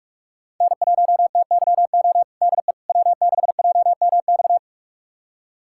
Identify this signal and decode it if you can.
Morse code, heard as a keyed tone, about 42 wpm, 700 Hz. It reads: N1TYO DE W6JMX
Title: USA Callsign DE Callsign - 5 Characters - 45wpm